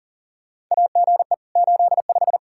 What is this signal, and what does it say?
Morse code, heard as a keyed tone, about 40 wpm, 700 Hz. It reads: AGE 85
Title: Practice QSO - Exchange 3 - More Info - 40wpm